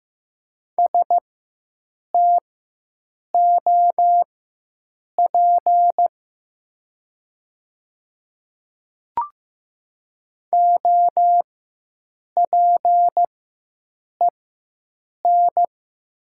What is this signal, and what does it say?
Morse code, heard as a keyed tone, about 15 wpm, spaced as for 7 wpm, 700 Hz. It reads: STOP OPEN
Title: 4x Character Spacing - Top 300 Words - 15wpm